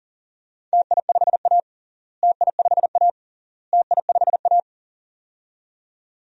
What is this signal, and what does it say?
Morse code, heard as a keyed tone, about 40 wpm, 700 Hz. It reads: TI5A TI5A TI5A
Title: DXCC-100 Call Signs - 4 Characters - Repeated 3x - 40wpm